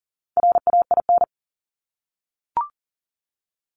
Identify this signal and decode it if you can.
Morse code, heard as a keyed tone, about 39 wpm, 690 Hz. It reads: RAIN
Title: Multi-letter Phonemes 40wpm